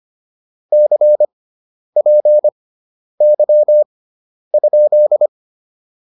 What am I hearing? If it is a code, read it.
Morse code, heard as a keyed tone, about 25 wpm, spaced as for 10 wpm, 600 Hz. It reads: CPY?
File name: QSO207 10